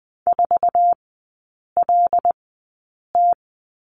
Morse code, heard as a keyed tone, about 20 wpm, spaced as for 8 wpm, 700 Hz. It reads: 4LT
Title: 074 Lesson 13e - Letter L - Three Characters 20wpm